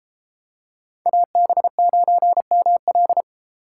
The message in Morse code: AB9ML

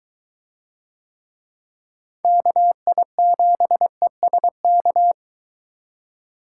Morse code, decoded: KI7ESK

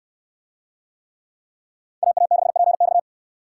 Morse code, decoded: AA6PX

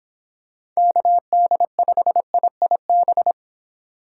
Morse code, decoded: KD5IIB

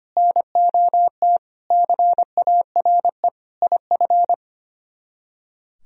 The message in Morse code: NOT CARE IF

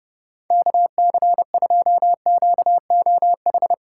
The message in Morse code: KC2QOH

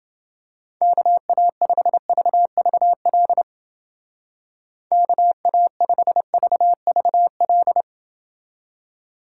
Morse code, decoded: KA5VVL KA5VVL

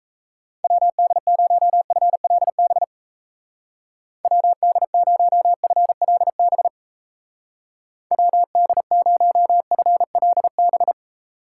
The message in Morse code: WD0FLB WD0FLB WD0FLB

